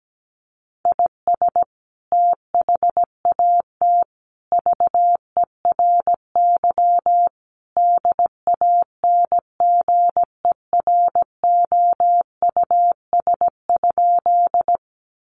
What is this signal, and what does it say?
Morse code, heard as a keyed tone, about 17 wpm, 700 Hz. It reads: IS THAT VERY DANGEROUS?
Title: Sentences from Top 1200 Words 17wpm